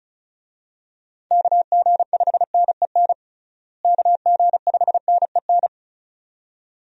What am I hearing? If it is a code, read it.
Morse code, heard as a keyed tone, about 34 wpm, 700 Hz. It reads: KG5NEN KG5NEN
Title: USA Call Signs - 6 Characters - Repeated 2x - 35wpm